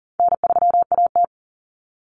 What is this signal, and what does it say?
Morse code, heard as a keyed tone, about 40 wpm, 700 Hz. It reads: N3AT
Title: Common Cut Numbers - 4-Digit 40wpm